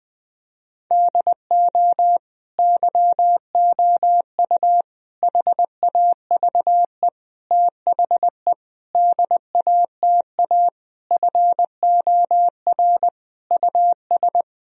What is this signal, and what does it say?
Morse code, heard as a keyed tone, about 20 wpm, 700 Hz. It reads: DO YOU HAVE THE DATA FOR US
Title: Sentences from Top 1000 Words - Review Part 3 - 20wpm